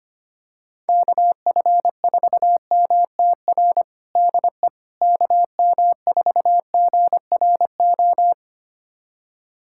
Morse code, heard as a keyed tone, about 25 wpm, 700 Hz. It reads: KF4MTR DE KM4GRO
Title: USA Callsign DE Callsign 25wpm